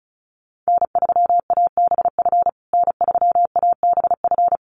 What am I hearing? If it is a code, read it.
Morse code, heard as a keyed tone, about 34 wpm, 700 Hz. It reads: N3ABF N3ABF
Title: USA Call Signs Repeated 2x 35wpm